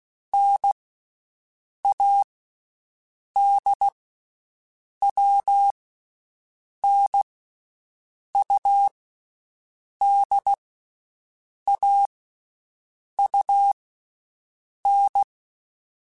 Morse code, heard as a keyed tone, about 16 wpm, 800 Hz. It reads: N A D W N U D A U N
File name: Basic Sets / Basic Letter Groups Collective7 / AWUNDG@13